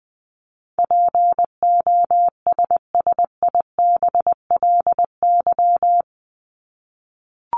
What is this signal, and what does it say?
Morse code, heard as a keyed tone, about 20 wpm, 700 Hz. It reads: POSSIBLY
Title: LY Suffix - Top 100 Words with Base Words 20wpm